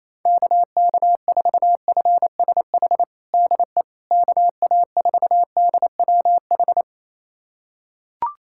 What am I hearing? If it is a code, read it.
Morse code, heard as a keyed tone, about 28 wpm, 700 Hz. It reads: KK4FSH DE KA4DWH
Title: USA Callsign DE Callsign - 6 Characters - 28wpm